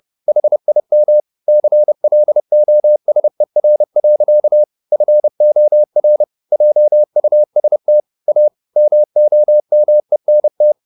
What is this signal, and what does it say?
Morse code, heard as a keyed tone, about 30 wpm, 600 Hz. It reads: HIM CLOSER. FOR JUST A MOMENT